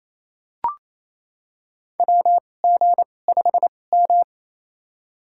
Morse code, heard as a keyed tone, about 27 wpm, spaced as for 20 wpm, 700 Hz. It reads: WG5M
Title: USA Call Signs - 4 Characters - 2x Character Spacing - 28wpm